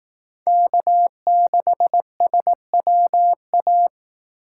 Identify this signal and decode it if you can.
Morse code, heard as a keyed tone, about 18 wpm, 700 Hz. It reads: K6SWA
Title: Speed Racing - USA 1x3 Call Signs 15wpm